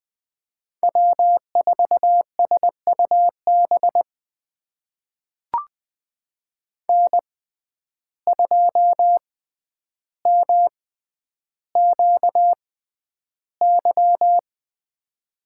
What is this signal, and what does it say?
Morse code, heard as a keyed tone, about 20 wpm, spaced as for 10 wpm, 700 Hz. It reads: W4SUB N2MQY
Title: USA Call Signs - 5 Characters - 6x Character Spacing - 20wpm